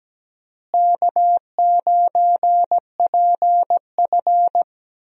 Morse code, heard as a keyed tone, about 17 wpm, 700 Hz. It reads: K9PF
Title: USA Call Signs - 4 Characters - Repeated 3x - 17wpm